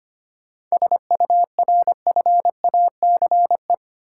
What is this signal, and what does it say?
Morse code, heard as a keyed tone, about 25 wpm, 700 Hz. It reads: SURFACE
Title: Sentences from Top 600 Words - New Word Drill - 25wpm